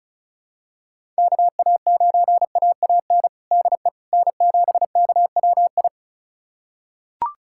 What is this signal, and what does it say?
Morse code, heard as a keyed tone, about 34 wpm, 700 Hz. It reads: KA9AAN DE N7KWI
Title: USA Callsign DE Callsign 35wpm